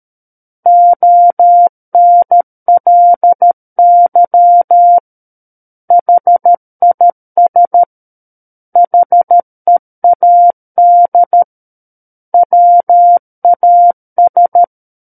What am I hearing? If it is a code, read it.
Morse code, heard as a keyed tone, about 13 wpm, 700 Hz. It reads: ONLY HIS HEAD WAS
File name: Book One / 04-Chapter 4 - 13 WPM